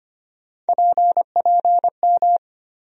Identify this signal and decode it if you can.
Morse code, heard as a keyed tone, about 25 wpm, 700 Hz. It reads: PPM
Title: USA Call Sign Postfixes - 3 Letters 25wpm